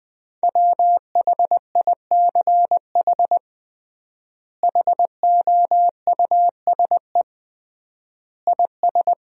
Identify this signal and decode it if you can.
Morse code, heard as a keyed tone, about 20 wpm, 700 Hz. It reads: WHICH HOUSE IS
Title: 3x Word-Spacing - Sentences from Top 200 Words - 20wpm